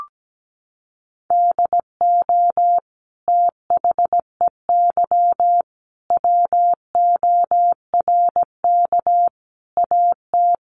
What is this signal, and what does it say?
Morse code, heard as a keyed tone, about 17 wpm, 700 Hz. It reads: DO THEY WORK AT
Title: Sentences from Top 1300 Words 17wpm